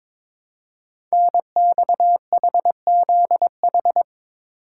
Morse code, heard as a keyed tone, about 22 wpm, 700 Hz. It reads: NXHZH
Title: Code Groups - 5 Letters - 22wpm